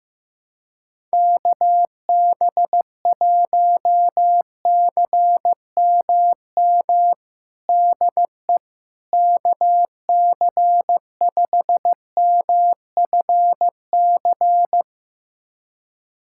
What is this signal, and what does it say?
Morse code, heard as a keyed tone, about 16 wpm, 700 Hz. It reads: KB1CMM DE KC5MFC
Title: USA Callsign DE Callsign - 6 Characters - 15wpm